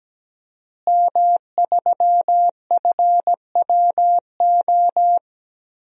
Morse code, heard as a keyed tone, about 17 wpm, 700 Hz. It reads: M3FWO
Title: DXCC-100 Call Signs - 5 Characters - Repeated 3x - 17wpm